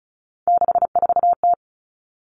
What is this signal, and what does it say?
Morse code, heard as a keyed tone, about 34 wpm, 700 Hz. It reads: 64T
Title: Common Cut Numbers - 3-Digit 35wpm